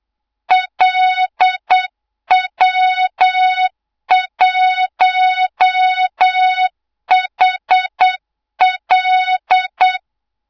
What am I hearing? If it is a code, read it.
Morse code, heard as a keyed tone, about 8 wpm, 3700 Hz, with noise. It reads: LW1HL
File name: Morze-HA2MN / hang / 24 GYAKORLAT